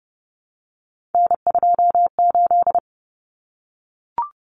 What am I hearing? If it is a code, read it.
Morse code, heard as a keyed tone, about 29 wpm, 700 Hz. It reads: N28
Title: Common Cut Numbers - 3-Digit 30wpm